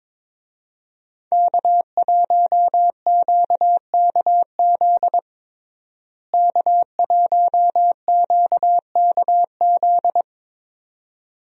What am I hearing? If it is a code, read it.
Morse code, heard as a keyed tone, about 22 wpm, 700 Hz. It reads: K1QKZ K1QKZ
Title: USA Call Signs - 5 Characters - Repeated 2x - 22wpm